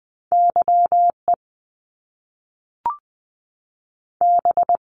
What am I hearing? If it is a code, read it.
Morse code, heard as a keyed tone, about 20 wpm, 700 Hz. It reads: Y E B
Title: Top Two Letter Combinations in Words 20wpm